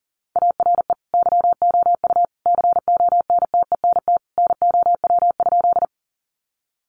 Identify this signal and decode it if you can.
Morse code, heard as a keyed tone, about 40 wpm, 700 Hz. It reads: ARE YOU CONTENT NOW?
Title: Sentences from Top 900 Words 40wpm